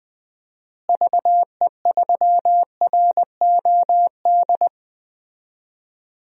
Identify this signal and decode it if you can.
Morse code, heard as a keyed tone, about 20 wpm, 700 Hz. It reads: VE3ROD
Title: DXCC-100 Call Signs - 6 Characters - Repeated 3x - 20wpm